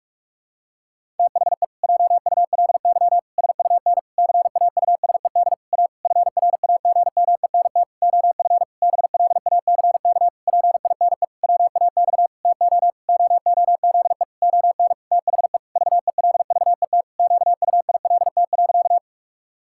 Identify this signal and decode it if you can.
Morse code, heard as a keyed tone, about 41 wpm, 700 Hz. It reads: THE JULY SUN CAUSED A FRAGMENT OF BLACK PINE WAX TO OOZE ON THE VELVET QUILT.
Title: Pangram Challenge v2 - 45wpm